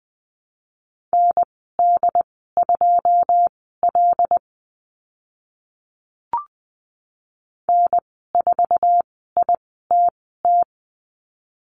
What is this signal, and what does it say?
Morse code, heard as a keyed tone, about 20 wpm, spaced as for 14 wpm, 700 Hz. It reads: ND2L N4ITT
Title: 082 Lesson 14g - Number 2 - Call Signs 20wpm